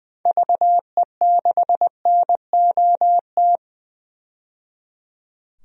Morse code, heard as a keyed tone, about 20 wpm, 700 Hz. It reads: VE6NOT
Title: DXCC-100 Call Signs - 6 Characters - 20wpm